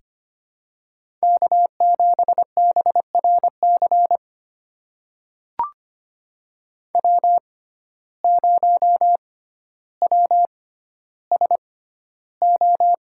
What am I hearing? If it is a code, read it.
Morse code, heard as a keyed tone, about 25 wpm, spaced as for 13 wpm, 700 Hz. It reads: K7BRC W0WSO
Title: USA Call Signs - 5 Characters - 6x Character Spacing - 25wpm